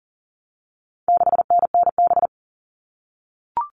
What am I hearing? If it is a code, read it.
Morse code, heard as a keyed tone, about 40 wpm, 700 Hz. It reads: 6NNB
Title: All Cut Numbers - 4-Digit 40wpm